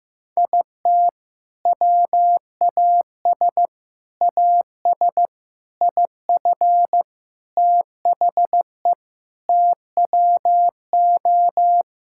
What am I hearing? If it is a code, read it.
Morse code, heard as a keyed tone, about 15 wpm, 700 Hz. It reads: IT WAS AS IF THE TWO
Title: Sentences from Top 1000 Words - Review Part 6 - 15wpm